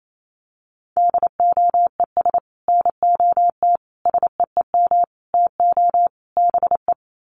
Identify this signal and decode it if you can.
Morse code, heard as a keyed tone, about 28 wpm, 700 Hz. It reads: DOES NOT SEEM TO BE
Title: Sets of 5 Words 28wpm